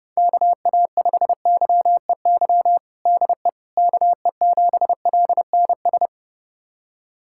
Morse code, heard as a keyed tone, about 29 wpm, 700 Hz. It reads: KA5YEY DE KE7LNS
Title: USA Callsign DE Callsign 30wpm